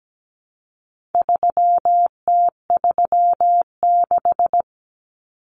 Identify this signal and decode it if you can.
Morse code, heard as a keyed tone, about 17 wpm, 700 Hz. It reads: 3T36
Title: Common Cut Numbers - 4-Digit 17wpm